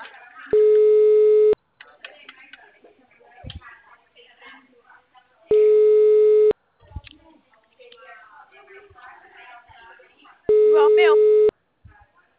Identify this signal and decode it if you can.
Ringback tone (Europe, 425 Hz), with speech.